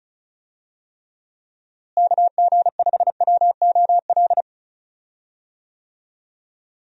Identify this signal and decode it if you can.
Morse code, heard as a keyed tone, about 34 wpm, 700 Hz. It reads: KG5WOL
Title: USA Call Signs - 6 Characters - Repeated 3x - 35wpm